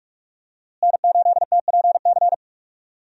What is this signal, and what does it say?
Morse code, heard as a keyed tone, about 46 wpm, 700 Hz. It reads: N8TPC
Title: USA Call Signs - 5 Characters - Repeated 2x - 45wpm